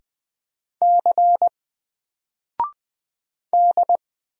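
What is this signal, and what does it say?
Morse code, heard as a keyed tone, about 20 wpm, 700 Hz. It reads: C D
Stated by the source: Single letter and number Morse drill at 20 wpm, from N4LCD.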